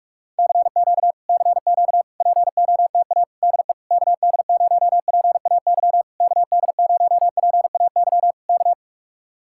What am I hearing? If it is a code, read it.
Morse code, heard as a keyed tone, about 49 wpm, 700 Hz. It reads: CQ CQ POTA DE KD0PAY KD0PAY K